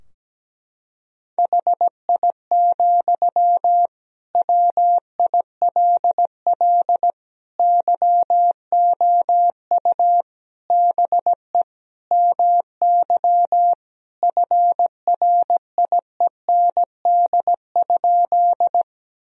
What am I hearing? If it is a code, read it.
Morse code, heard as a keyed tone, about 17 wpm, 700 Hz. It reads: HI, WILL YOU BE MY FRIEND?
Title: Sentences from Top 1000 Words - Review Part 3 - 17wpm